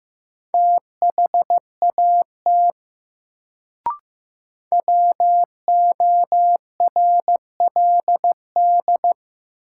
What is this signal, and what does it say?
Morse code, heard as a keyed tone, about 15 wpm, 700 Hz. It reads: THAT WORLD
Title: Sets of 2 Words - New Word Drill - 15wpm